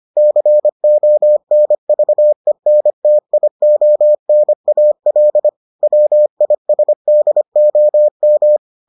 Morse code, heard as a keyed tone, about 25 wpm, 600 Hz. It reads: CONVENTIONAL WISDOM